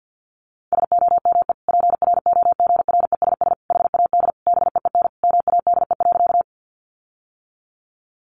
Morse code, heard as a keyed tone, about 50 wpm, 700 Hz. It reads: SOME PROGRESS HAD BEEN MADE.
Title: Sentences from Top 1400 Words 50wpm